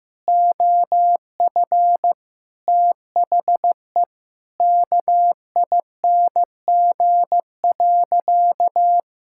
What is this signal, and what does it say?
Morse code, heard as a keyed tone, about 15 wpm, 700 Hz. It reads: OF THE KING.
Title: Sentences from Top 1000 Words - Review Part 5 - 15wpm